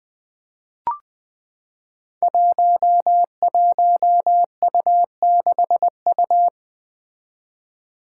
Morse code, heard as a keyed tone, about 20 wpm, 700 Hz. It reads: 11U6U
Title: Five Challenging Letter-Number Combinations 20wpm